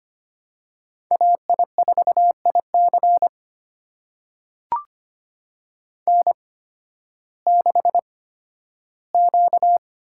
Morse code, heard as a keyed tone, about 25 wpm, spaced as for 13 wpm, 700 Hz. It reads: AI4IC N6Q
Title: USA Call Signs - 5 Characters - 8x Character Spacing - 25wpm